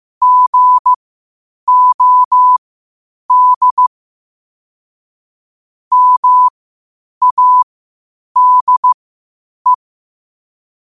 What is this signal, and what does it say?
Morse code, heard as a keyed tone, about 15 wpm, spaced as for 8 wpm, 1000 Hz. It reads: GOD MADE
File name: qotdfiles / 8wpm